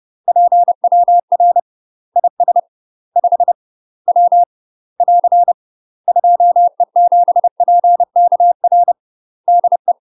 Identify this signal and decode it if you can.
Morse code, heard as a keyed tone, about 30 wpm, 700 Hz. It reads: PWR IS 5 W <AR> 2E7PKR DE
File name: UK MASTER QSOs TRAINING / 30WPM / UK MASTER QSOs 0031